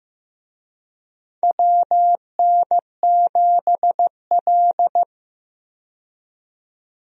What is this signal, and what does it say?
Morse code, heard as a keyed tone, about 15 wpm, 700 Hz. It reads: WN7L
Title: USA Call Signs - 4 Characters - 15wpm